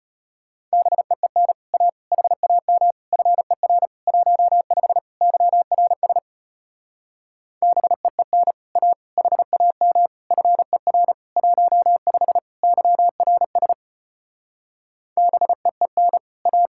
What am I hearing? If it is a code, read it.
Morse code, heard as a keyed tone, about 33 wpm, 700 Hz. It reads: BEEN A HAM FER 15 YRS BEEN A HAM FER 15 YRS BEEN A